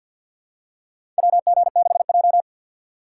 Morse code, heard as a keyed tone, about 53 wpm, 700 Hz. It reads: WG6J
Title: USA Call Signs - 4x Character Spacing - 50wpm